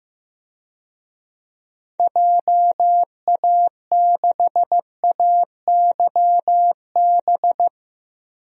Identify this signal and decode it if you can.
Morse code, heard as a keyed tone, about 15 wpm, 700 Hz. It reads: JA6AYB
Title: DXCC-100 Call Signs - 6 Characters - Repeated 2x - 15wpm